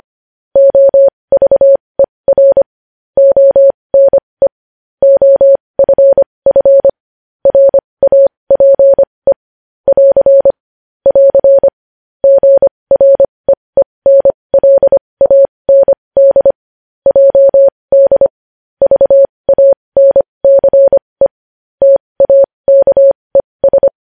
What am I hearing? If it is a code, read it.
Morse code, heard as a keyed tone, about 25 wpm, spaced as for 20 wpm, 550 Hz. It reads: OVER ONE OFF RAPE <AR> <AR> GREENLAND JD VANCE TAKES
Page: News Headlines in Morse Code at 20 WPM